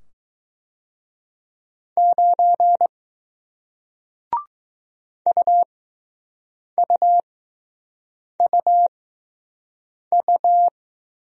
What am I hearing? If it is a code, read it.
Morse code, heard as a keyed tone, about 20 wpm, spaced as for 6 wpm, 700 Hz. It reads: 9 UUUU